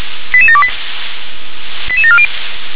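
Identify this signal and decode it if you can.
Five-tone selective call (ZVEI), heard as quick runs of five tones, tones of about 70 ms. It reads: 88728 88740
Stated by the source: ZVEI five-tone call 88730 received by SDR.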